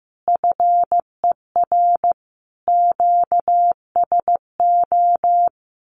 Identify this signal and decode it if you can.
Morse code, heard as a keyed tone, about 15 wpm, 700 Hz. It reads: FER QSO
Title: Common QSO Phrases Encore 15wpm